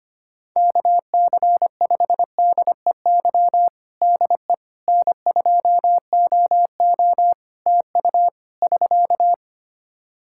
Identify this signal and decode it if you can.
Morse code, heard as a keyed tone, about 25 wpm, 700 Hz. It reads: KC5DEY DE N2OO TU <SK>
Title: K1USN SST QSO Practice 25wpm